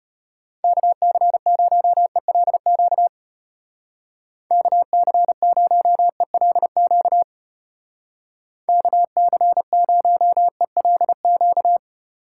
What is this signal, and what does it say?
Morse code, heard as a keyed tone, about 30 wpm, 700 Hz. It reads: KC0ELQ KC0ELQ KC0ELQ
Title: Speed Racing - USA 2x3 Call Signs 25wpm